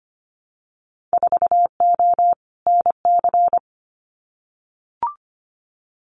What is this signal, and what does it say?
Morse code, heard as a keyed tone, about 25 wpm, 700 Hz. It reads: 4O NC
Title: Winter Field Day Exchanges Encore 25wpm